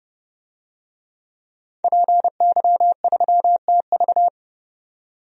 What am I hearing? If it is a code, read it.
Morse code, heard as a keyed tone, about 30 wpm, 700 Hz. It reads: PY3TV
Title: DXCC-100 Call Signs - 5 Characters - 30wpm